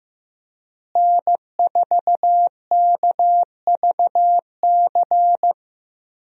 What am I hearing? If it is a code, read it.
Morse code, heard as a keyed tone, about 15 wpm, 700 Hz. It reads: N4KVC
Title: USA Call Signs - 5 Characters - Repeated 3x - 15wpm